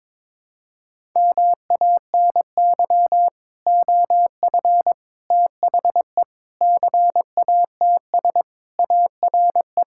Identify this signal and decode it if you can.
Morse code, heard as a keyed tone, about 22 wpm, 700 Hz. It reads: MANY OF THE CATS ARE
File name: practice / 22wpm / Sentences with 3-Word Phrases - 22wpm